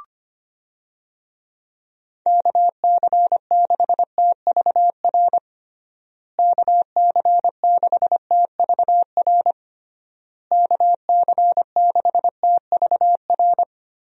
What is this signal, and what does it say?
Morse code, heard as a keyed tone, about 25 wpm, 700 Hz. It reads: KC6TVR KC6TVR KC6TVR
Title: USA Call Signs - 6 Characters - Repeated 3x - 25wpm